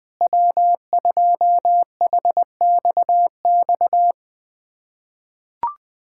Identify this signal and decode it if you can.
Morse code, heard as a keyed tone, about 20 wpm, 700 Hz. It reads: W2HXX
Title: USA Callsign DE Callsign - 5 Characters - 20wpm